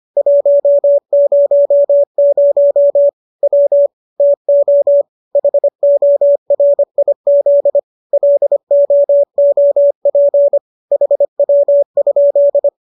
Morse code, heard as a keyed tone, about 25 wpm, 570 Hz. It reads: 100 W TO HORIZ LOOP HW?